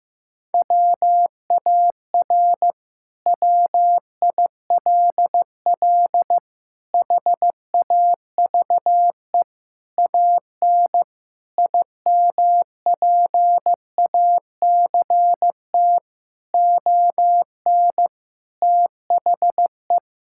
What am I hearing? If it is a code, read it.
Morse code, heard as a keyed tone, about 16 wpm, 700 Hz. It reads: WAR WILL HAVE AN IMPACT ON THE
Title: Sentences from Top 1000 Words - Review Part 6 - 15wpm